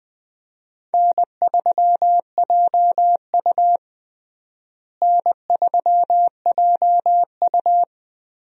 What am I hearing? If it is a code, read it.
Morse code, heard as a keyed tone, about 20 wpm, 700 Hz. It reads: N3JU N3JU
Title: USA Call Signs - 4 Characters - Repeated 3x - 20wpm